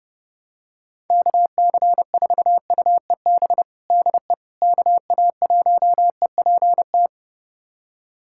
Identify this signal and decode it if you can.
Morse code, heard as a keyed tone, about 32 wpm, 700 Hz. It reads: KC4UEB DE KA1EPT
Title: USA Callsign DE Callsign - 6 Characters - 30wpm